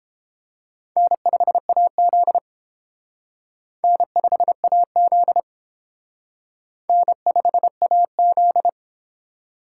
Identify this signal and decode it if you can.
Morse code, heard as a keyed tone, about 28 wpm, 700 Hz. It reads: N5AZ N5AZ N5AZ